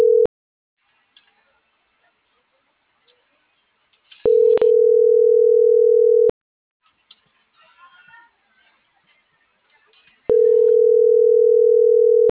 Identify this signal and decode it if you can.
Ringback tone (North America), 440 + 480 Hz.